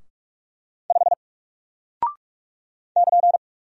Morse code, heard as a keyed tone, about 44 wpm, 700 Hz. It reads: 5 <KN>
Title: Single Character (Letter:Number:Punctuation:Prosign) Alt 45wpm